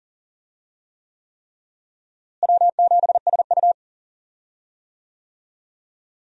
Morse code, heard as a keyed tone, about 40 wpm, 700 Hz. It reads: W7SU
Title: USA Call Signs - 4 Characters - Repeated 3x - 40wpm